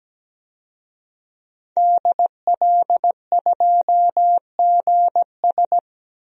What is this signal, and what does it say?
Morse code, heard as a keyed tone, about 17 wpm, 700 Hz. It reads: DL2GS
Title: DXCC-100 Call Signs - 5 Characters - Repeated 3x - 17wpm